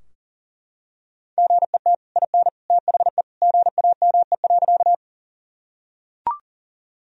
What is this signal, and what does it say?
Morse code, heard as a keyed tone, about 40 wpm, 700 Hz. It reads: GET IN THE GAME.
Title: Sentences from Top 1000 Words - Review Part 3 - 40wpm